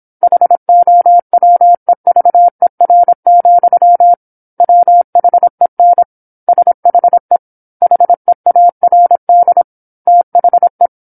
Morse code, heard as a keyed tone, about 27 wpm, 700 Hz. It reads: HOWEVER, WHEN SHE HEARD THE